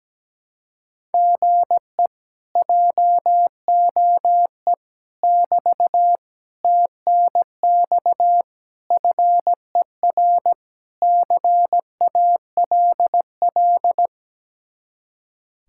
Morse code, heard as a keyed tone, about 17 wpm, 700 Hz. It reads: GE JOE <BT> TNX FER CALL
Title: Speed-Racing Common QSO Phrases Encore - 17wpm